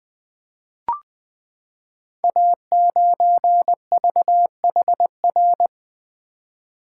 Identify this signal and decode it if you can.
Morse code, heard as a keyed tone, about 20 wpm, 700 Hz. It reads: A9VHR
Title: Code Groups - 5 Letter:Numbers - 20wpm